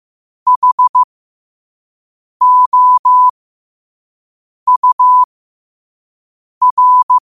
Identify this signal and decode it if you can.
Morse code, heard as a keyed tone, about 15 wpm, spaced as for 5 wpm, 1000 Hz. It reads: HOUR